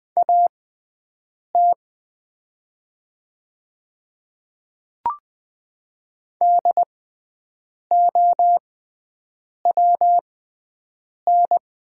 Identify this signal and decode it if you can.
Morse code, heard as a keyed tone, about 20 wpm, spaced as for 7 wpm, 700 Hz. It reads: AT DOWN
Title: 6x Character Spacing - Top 200 Words - 20wpm